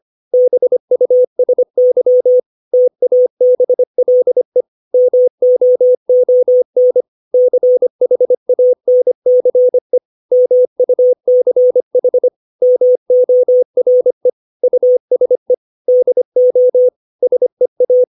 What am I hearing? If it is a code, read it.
Morse code, heard as a keyed tone, about 25 wpm, 500 Hz. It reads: BUSY TABLE MOON CHANCE MUCH MORE USE DO SEA
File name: 632 words / 632 Words 25WPM